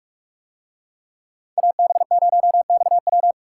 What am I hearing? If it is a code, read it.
Morse code, heard as a keyed tone, about 46 wpm, 700 Hz. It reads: AB0XW